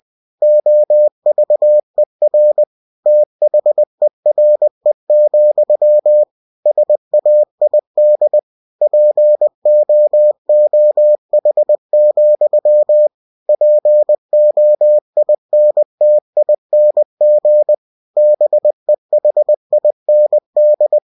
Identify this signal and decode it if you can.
Morse code, heard as a keyed tone, about 20 wpm, 600 Hz. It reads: OVER THERE, SAID POOH, POINTING BEHIND
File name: cw-books / winnie-the-pooh / winnie-20wpm-page-0069